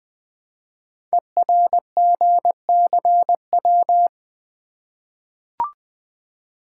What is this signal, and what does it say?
Morse code, heard as a keyed tone, about 20 wpm, 700 Hz. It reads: ERGCW
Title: Code Groups - 5 Letter:Numbers - 20wpm